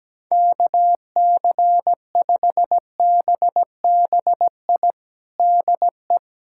Morse code, heard as a keyed tone, about 17 wpm, 700 Hz. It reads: KC5BBI DE
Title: USA Callsign DE Callsign - 6 Characters - 17wpm